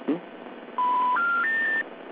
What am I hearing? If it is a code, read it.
Special information tone, with speech.
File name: recordings / RE1e0f12afb3e86e c34357d